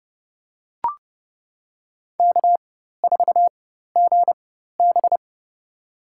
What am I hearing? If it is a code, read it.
Morse code, heard as a keyed tone, about 30 wpm, spaced as for 14 wpm, 700 Hz. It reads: K4GB
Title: USA Call Signs - 4 Characters - 4x Character Spacing - 30wpm